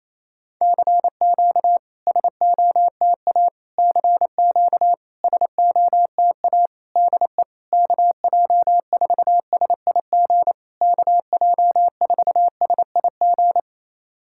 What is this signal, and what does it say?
Morse code, heard as a keyed tone, about 28 wpm, 700 Hz. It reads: CQ SOTA CQ SOTA DE KJ4SIG KJ4SIG
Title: SOTA QSO Practice 28wpm